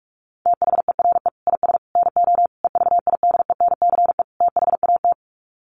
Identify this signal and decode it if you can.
Morse code, heard as a keyed tone, about 44 wpm, 700 Hz. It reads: THERE IS NO EVIDENCE THAT
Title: Sets of 5 Words Encore 45wpm